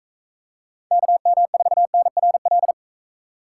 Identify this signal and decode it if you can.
Morse code, heard as a keyed tone, about 42 wpm, 700 Hz. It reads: KM4NRL